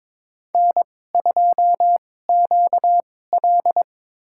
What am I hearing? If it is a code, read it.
Morse code, heard as a keyed tone, about 22 wpm, spaced as for 16 wpm, 700 Hz. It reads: N2QL
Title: USA Call Signs - 4 Characters - 2x Character Spacing - 22wpm